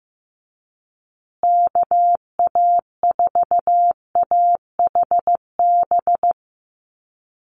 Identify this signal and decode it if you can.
Morse code, heard as a keyed tone, about 15 wpm, 700 Hz. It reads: KA4AHB
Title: USA Call Signs Repeated 2x 15wpm